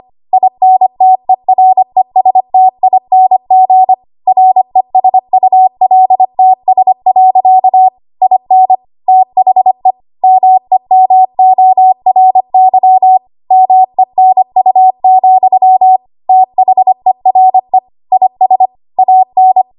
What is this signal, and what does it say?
Morse code, heard as a keyed tone, about 25 wpm, 750 Hz. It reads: INTERESTING RESULTS. IN THE MEMORY MENU, THERE IS AN
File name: Archive / 25 WPM / 25WPM